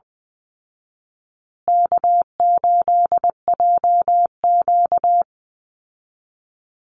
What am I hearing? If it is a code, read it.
Morse code, heard as a keyed tone, about 20 wpm, 700 Hz. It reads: K8JQ
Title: Easy Call Signs Repeated 3x - 20wpm